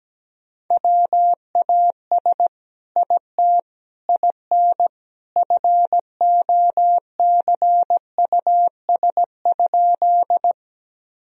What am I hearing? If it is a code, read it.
Morse code, heard as a keyed tone, about 17 wpm, 700 Hz. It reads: WAS IT IN FOCUS?
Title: Sentences from Top 1000 Words - Review Part 4 - 17wpm